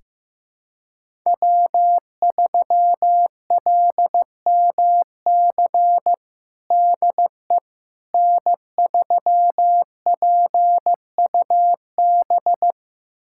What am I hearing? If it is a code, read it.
Morse code, heard as a keyed tone, about 15 wpm, 700 Hz. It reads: W3LMC DE N3PUB
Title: USA Callsign DE Callsign - 5 Characters - 15wpm